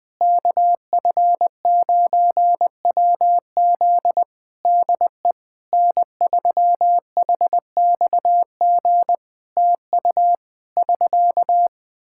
Morse code, heard as a keyed tone, about 20 wpm, 700 Hz. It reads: KF9WZ DE N3HXG TU <SK>
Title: Practice QSO - Exchange 4 - Ending - 20wpm